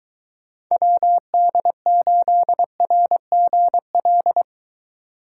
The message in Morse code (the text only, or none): WD8RGL